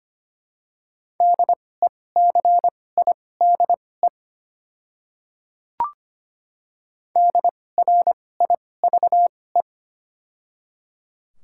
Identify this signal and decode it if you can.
Morse code, heard as a keyed tone, about 25 wpm, spaced as for 18 wpm, 700 Hz. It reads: DECIDE DRIVE
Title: 2x Character Spacing - Top 500 Words - 25wpm